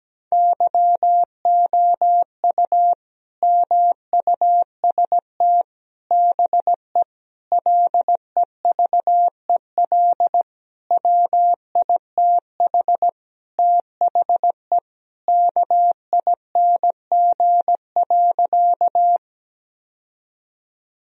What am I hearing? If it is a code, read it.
Morse code, heard as a keyed tone, about 17 wpm, 700 Hz. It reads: YOU MUST BE LEVEL WITH THE KING.
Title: Sentences from Top 1000 Words - Review Part 4 - 17wpm